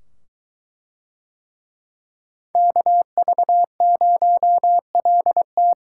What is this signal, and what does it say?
Morse code, heard as a keyed tone, about 21 wpm, 700 Hz. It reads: KV0LT